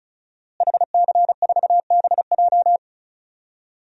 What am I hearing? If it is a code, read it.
Morse code, heard as a keyed tone, about 36 wpm, 700 Hz. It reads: HC4BJ
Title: DXCC-100 Call Signs - 5 Characters - Repeated 2x - 35wpm